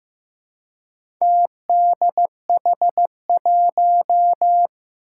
Morse code, heard as a keyed tone, about 15 wpm, 700 Hz. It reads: TDH1
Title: Four Challenging Letter-Number Combinations 15wpm